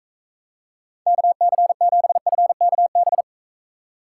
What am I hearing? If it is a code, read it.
Morse code, heard as a keyed tone, about 42 wpm, 700 Hz. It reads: KC7FKB